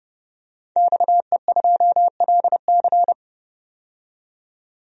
Morse code, heard as a keyed tone, about 30 wpm, 700 Hz. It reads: XE2LC
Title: DXCC-100 Call Signs - 5 Characters - 30wpm